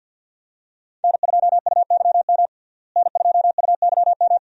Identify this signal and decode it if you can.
Morse code, heard as a keyed tone, about 50 wpm, 700 Hz. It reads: N2UYM N2UYM
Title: USA Call Signs - 5 Characters - Repeated 2x - 50wpm